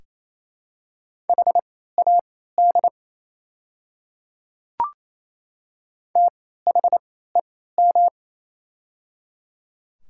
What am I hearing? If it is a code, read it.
Morse code, heard as a keyed tone, about 28 wpm, spaced as for 16 wpm, 700 Hz. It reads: HAD THEM
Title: 3x Character Spacing - Top 100 Words - 28wpm